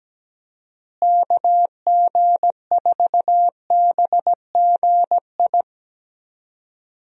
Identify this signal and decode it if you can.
Morse code, heard as a keyed tone, about 17 wpm, 700 Hz. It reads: KG4BGI
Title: USA Call Signs - 6 Characters - Repeated 2x - 17wpm